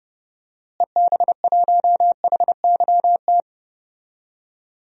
Morse code, heard as a keyed tone, about 30 wpm, 700 Hz. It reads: EB1HYT
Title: DXCC-100 Call Signs - 6 Characters - Repeated 3x - 30wpm